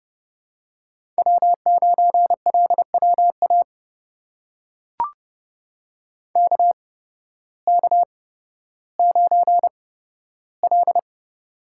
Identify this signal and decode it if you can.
Morse code, heard as a keyed tone, about 30 wpm, spaced as for 14 wpm, 700 Hz. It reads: W9LWA KK9L
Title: USA Call Signs - 8x Character Spacing - 30wpm